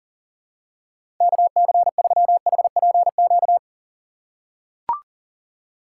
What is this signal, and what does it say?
Morse code, heard as a keyed tone, about 40 wpm, 700 Hz. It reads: KC3HPQ
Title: USA Call Signs - 6 Characters - 2x Character Spacing - 40wpm